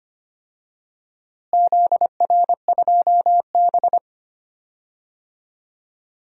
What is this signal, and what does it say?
Morse code, heard as a keyed tone, about 25 wpm, 700 Hz. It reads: ZR2B
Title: DXCC-100 Call Signs - 4 Characters - Repeated 2x - 25wpm